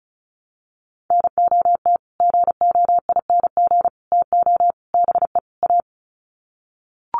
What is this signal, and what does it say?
Morse code, heard as a keyed tone, about 34 wpm, 700 Hz. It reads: NOT GOING TO BE A
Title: Sets of 5 Words Encore 35wpm